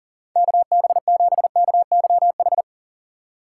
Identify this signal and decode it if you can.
Morse code, heard as a keyed tone, about 40 wpm, 700 Hz. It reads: KB7KYH
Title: USA Call Signs - 6x Character Spacing - 40wpm